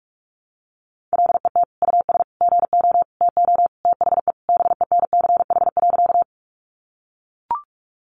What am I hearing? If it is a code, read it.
Morse code, heard as a keyed tone, about 44 wpm, 700 Hz. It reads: LET US GO TO THE BENCH.
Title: Sentences from Top 1400 Words 45wpm